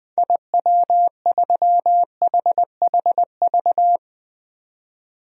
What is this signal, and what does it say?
Morse code, heard as a keyed tone, about 20 wpm, 700 Hz. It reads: IW3HHV
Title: DXCC-100 Call Signs - 6 Characters - 20wpm